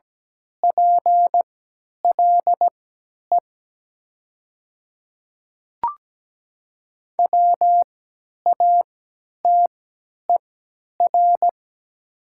Morse code, heard as a keyed tone, about 17 wpm, spaced as for 10 wpm, 700 Hz. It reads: PLE WATER
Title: 3x Character Spacing - Top 300 Words - 17wpm